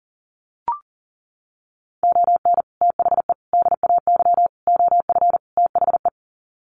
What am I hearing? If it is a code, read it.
Morse code, heard as a keyed tone, about 40 wpm, 700 Hz. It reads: ON THE DAY OF THE
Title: Sets of 5 Words Encore 40wpm